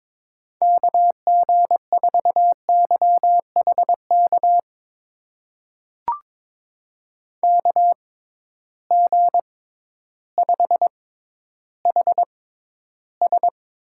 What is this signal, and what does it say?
Morse code, heard as a keyed tone, about 22 wpm, spaced as for 12 wpm, 700 Hz. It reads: KG4YHK KG5HS
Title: USA Call Signs - 6 Characters - 6x Character Spacing - 22wpm